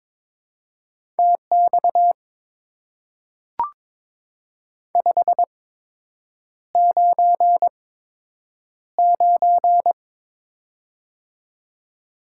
Morse code, heard as a keyed tone, about 22 wpm, spaced as for 8 wpm, 700 Hz. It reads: TX 599
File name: Top 90 QSO Elements - 8x Character Spacing - 22wpm